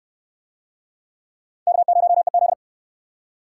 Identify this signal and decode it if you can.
Morse code, heard as a keyed tone, about 68 wpm, 700 Hz. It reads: K0EZ